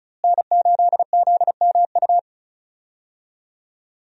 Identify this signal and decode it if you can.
Morse code, heard as a keyed tone, about 36 wpm, 700 Hz. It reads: N8ZMU
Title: USA Call Signs - 5 Characters - Repeated 3x - 35wpm